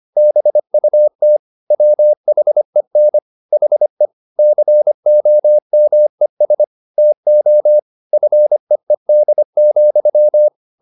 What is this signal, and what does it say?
Morse code, heard as a keyed tone, about 25 wpm, 600 Hz. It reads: BUT WHEN HE COMES TO FEED,